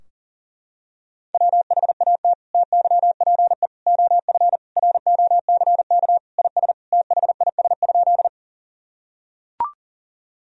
Morse code, heard as a keyed tone, about 40 wpm, 700 Hz. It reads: WHAT TYPE OF ROCK IS THIS?
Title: Sentences from Top 1000 Words - Review Part 3 - 40wpm